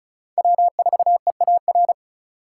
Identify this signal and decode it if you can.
Morse code, heard as a keyed tone, about 43 wpm, 700 Hz. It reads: W4EAR
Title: Speed Racing - USA 1x3 Call Signs 35wpm